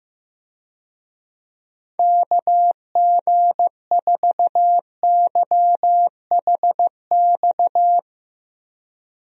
Morse code, heard as a keyed tone, about 15 wpm, 700 Hz. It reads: KG4YHX
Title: USA Call Signs - 6 Characters - Repeated 3x - 15wpm